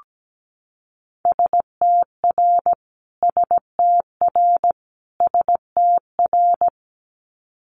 Morse code, heard as a keyed tone, about 17 wpm, 700 Hz. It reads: STR STR STR
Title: Multi-letter Phonemes Repeated 3x 17wpm